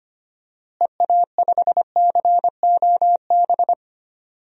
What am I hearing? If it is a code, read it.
Morse code, heard as a keyed tone, about 25 wpm, 700 Hz. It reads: EA5COB
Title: DXCC-100 Call Signs - 6 Characters - Repeated 3x - 25wpm